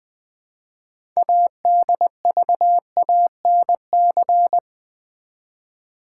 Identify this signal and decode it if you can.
Morse code, heard as a keyed tone, about 20 wpm, 700 Hz. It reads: ADVANC
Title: Build A Word - 2 Letters At A Time - 8-Letter Words - 20wpm